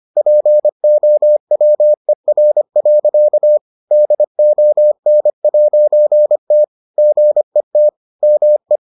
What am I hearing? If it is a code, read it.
Morse code, heard as a keyed tone, about 25 wpm, 600 Hz. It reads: POWER. DON'T GET ME